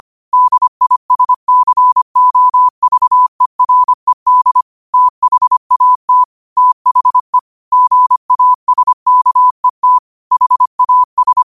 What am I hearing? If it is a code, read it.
Morse code, heard as a keyed tone, about 25 wpm, 1000 Hz. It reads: DISCOVERED THAT THE GASKET HAS
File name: qotdfiles / 25wpm